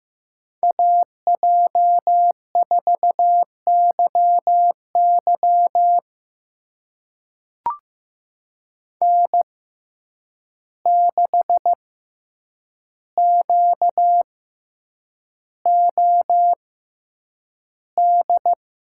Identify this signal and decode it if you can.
Morse code, heard as a keyed tone, about 15 wpm, spaced as for 8 wpm, 700 Hz. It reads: AJ4YY N6QOD
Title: USA Call Signs - 5 Characters - 6x Character Spacing - 15wpm